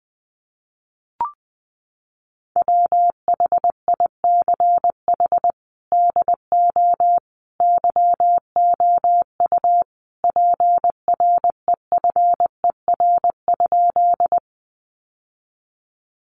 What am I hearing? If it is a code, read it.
Morse code, heard as a keyed tone, about 20 wpm, 700 Hz. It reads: WHICH DO YOU PREFER?
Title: Sentences from Top 1100 Words 20wpm